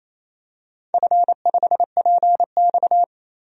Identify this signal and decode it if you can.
Morse code, heard as a keyed tone, about 28 wpm, 700 Hz. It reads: F5PX
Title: DXCC-100 Call Signs - 4 Characters - 28wpm